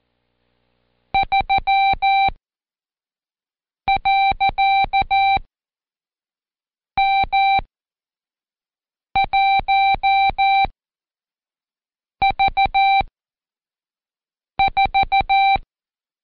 Morse code, heard as a keyed tone, about 14 wpm, 780 Hz. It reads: 3 . M 1 V 4
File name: Code Quick / AudioDownld1 / Seg74-Day8-Pract 1234